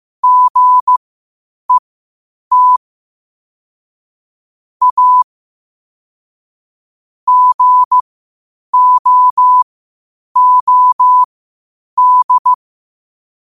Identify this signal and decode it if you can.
Morse code, heard as a keyed tone, about 15 wpm, spaced as for 8 wpm, 1000 Hz. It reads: GET A GOOD